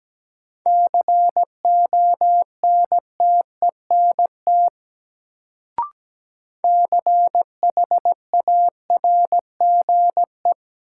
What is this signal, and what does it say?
Morse code, heard as a keyed tone, about 17 wpm, 700 Hz. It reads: CONTENT CHARGE
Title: Sentences from Top 900 Words - New Word Drill - 17wpm